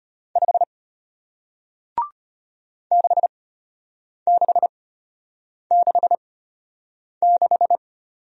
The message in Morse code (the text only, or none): 5 6666